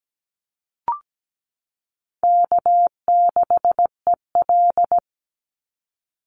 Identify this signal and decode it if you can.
Morse code, heard as a keyed tone, about 17 wpm, 700 Hz. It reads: K6EL
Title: NA SOTA Chaser Callsigns 17wpm